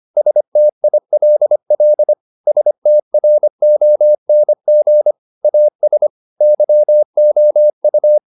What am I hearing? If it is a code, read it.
Morse code, heard as a keyed tone, about 25 wpm, 600 Hz. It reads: STILL STRONG AS YOU
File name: the old man and the sea part 1 25wpm